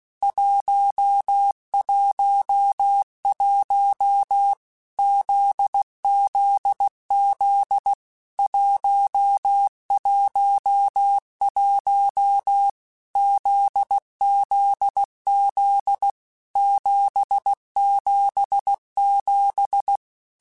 Morse code, heard as a keyed tone, about 16 wpm, 800 Hz. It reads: 111 ZZZ 111 ZZZ 777
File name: Basic Letter Groups By7Groups / JZ17 / JZ17@15